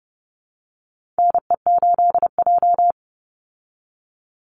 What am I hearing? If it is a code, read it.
Morse code, heard as a keyed tone, about 30 wpm, 700 Hz. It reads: NE8J
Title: Call Signs Repeated 3x - 30wpm